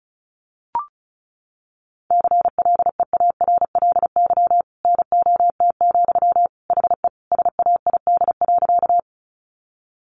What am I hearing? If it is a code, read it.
Morse code, heard as a keyed tone, about 34 wpm, 700 Hz. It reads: CLEARLY NOT, HE SAID.
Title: Sentences from Top 1300 Words 35wpm